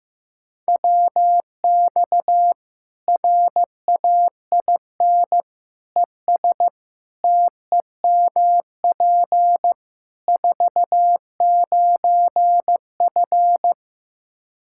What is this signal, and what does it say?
Morse code, heard as a keyed tone, about 15 wpm, 700 Hz. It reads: WX RAIN ES TEMP 49F
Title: Practice QSO - Exchange 2 - Info - 15wpm